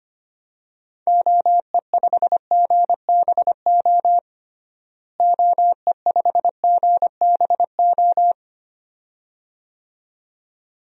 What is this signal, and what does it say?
Morse code, heard as a keyed tone, about 25 wpm, 700 Hz. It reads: OE5GBO OE5GBO
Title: DXCC-100 Call Signs - 6 Characters - Repeated 2x - 25wpm